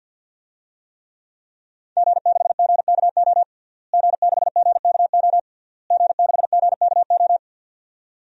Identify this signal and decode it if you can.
Morse code, heard as a keyed tone, about 46 wpm, 700 Hz. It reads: G6GKO G6GKO G6GKO
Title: DXCC-100 Call Signs - 5 Characters - Repeated 3x - 50wpm